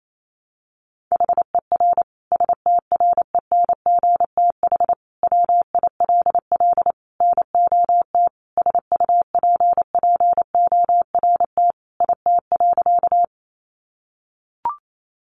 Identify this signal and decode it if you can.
Morse code, heard as a keyed tone, about 28 wpm, 700 Hz. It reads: HER STRENGTH WILL NOT SUPPORT IT.
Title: Sentences from Top 1300 Words 28wpm